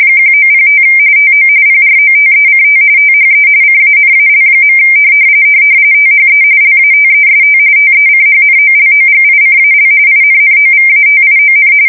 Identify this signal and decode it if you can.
RTTY, about 45.45 baud. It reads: 1905 TO PROTECT THE CANAL. THE LATEST TECHNOLOGICAL INNOVATION
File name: fsk170 / usnavy3